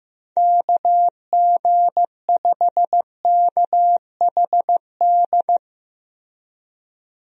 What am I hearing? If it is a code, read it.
Morse code, heard as a keyed tone, about 15 wpm, 700 Hz. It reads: KG5KHD
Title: USA Call Signs - 6 Characters - Repeated 3x - 15wpm